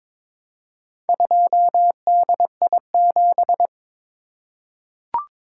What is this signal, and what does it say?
Morse code, heard as a keyed tone, about 22 wpm, 700 Hz. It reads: 2DI7
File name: practice / 22wpm / Four Letter-Number Combinations - 22wpm